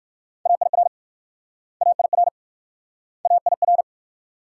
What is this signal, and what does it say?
Morse code, heard as a keyed tone, about 58 wpm, 700 Hz. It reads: AIR AIR AIR